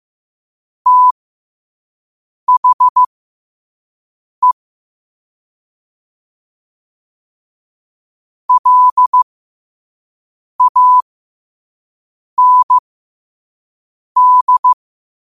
Morse code, heard as a keyed tone, about 15 wpm, spaced as for 5 wpm, 1000 Hz. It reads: THE LAND
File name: 5wpm